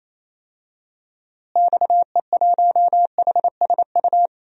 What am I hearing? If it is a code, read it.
Morse code, heard as a keyed tone, about 27 wpm, 700 Hz. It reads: XE1HSU
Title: DXCC-100 Call Signs - 6 Characters - Repeated 3x - 28wpm